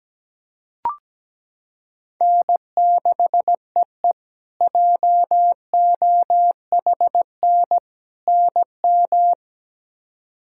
Morse code, heard as a keyed tone, about 17 wpm, 700 Hz. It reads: N6EE JOHN NM
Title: K1USN SST QSO Practice - 17wpm